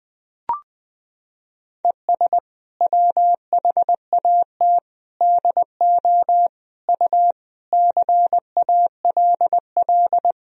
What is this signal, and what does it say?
Morse code, heard as a keyed tone, about 20 wpm, 700 Hz. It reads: ES WHAT DO U CALL
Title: Sentences from Top 100 Words and Abbreviations 20wpm